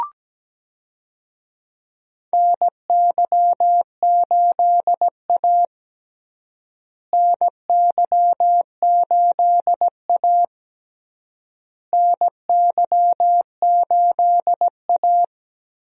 Morse code, heard as a keyed tone, about 17 wpm, 700 Hz. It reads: NY8A NY8A NY8A
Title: USA Call Signs - 4 Characters - Repeated 3x - 17wpm